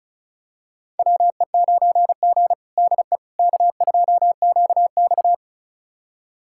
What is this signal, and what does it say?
Morse code, heard as a keyed tone, about 36 wpm, 700 Hz. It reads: WE9G DE K2QX
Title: USA Callsign DE Callsign - 4 Characters - 35wpm